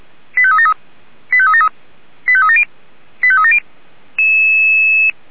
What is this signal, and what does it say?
Five-tone selective call (ZVEI), heard as quick runs of five tones, tones of about 80 ms. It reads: 85373 85373 85370 85370
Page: /464/SDR/ZVEI/89485/2024.05.06